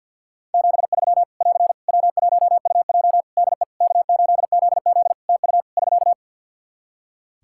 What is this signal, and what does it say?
Morse code, heard as a keyed tone, about 46 wpm, 700 Hz. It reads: 73 <AR> W1UJ DE K8ZZ TU <SK>
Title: Practice QSO - Exchange 4 - Ending - 50wpm